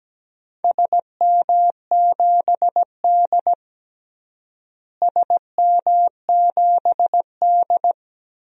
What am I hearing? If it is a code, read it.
Morse code, heard as a keyed tone, about 17 wpm, 700 Hz. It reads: SM7D SM7D
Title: DXCC-100 Call Signs - 4 Characters - Repeated 2x - 17wpm